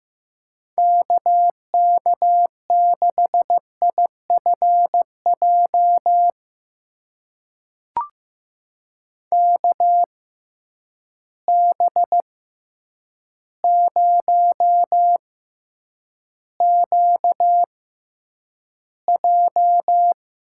Morse code, heard as a keyed tone, about 15 wpm, spaced as for 8 wpm, 700 Hz. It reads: KK6IFJ KB0QJ
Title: USA Call Signs - 6x Character Spacing - 15wpm